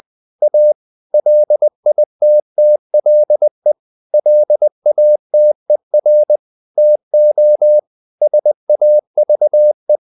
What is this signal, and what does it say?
Morse code, heard as a keyed tone, about 20 wpm, 600 Hz. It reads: A LITTLE LATER TO SAVE
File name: the old man and the sea part 2 20wpm